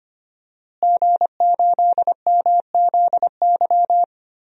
Morse code, heard as a keyed tone, about 25 wpm, 700 Hz. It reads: G8MZY
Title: DXCC-100 Call Signs - 5 Characters - Repeated 2x - 25wpm